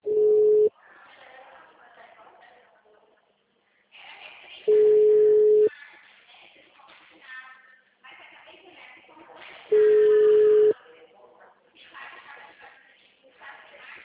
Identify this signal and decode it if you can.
Ringback tone (Europe, 425 Hz), with speech.